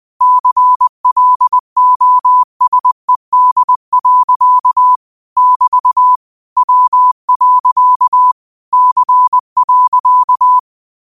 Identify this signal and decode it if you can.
Morse code, heard as a keyed tone, about 20 wpm, 1000 Hz. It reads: CLOSED. <BT> W. C.
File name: qotdfiles / 20wpm